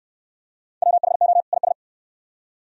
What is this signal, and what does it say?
Morse code, heard as a keyed tone, about 56 wpm, 700 Hz. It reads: WHO IS
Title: Speed-Racing - Sets of 2 Words 45wpm